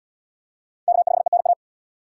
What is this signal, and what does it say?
Morse code, heard as a keyed tone, about 71 wpm, 700 Hz. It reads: K5ENA